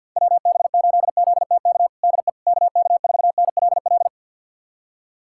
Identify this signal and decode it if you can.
Morse code, heard as a keyed tone, about 48 wpm, 700 Hz. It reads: WB8ZTK DE KK4NLL
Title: USA Callsign DE Callsign - 6 Characters - 50wpm